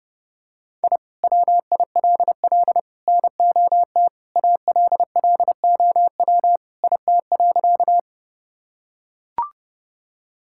Morse code, heard as a keyed tone, about 32 wpm, 700 Hz. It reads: I WILL NOT ALLOW IT.
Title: Sentences from Top 1000 Words - Review Part 5 - 30wpm